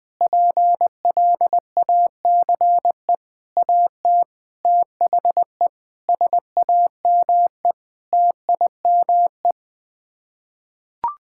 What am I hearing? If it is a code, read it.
Morse code, heard as a keyed tone, about 21 wpm, 700 Hz. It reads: PLACE AT THE SAME TIME
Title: Sets of 8 Words - 20wpm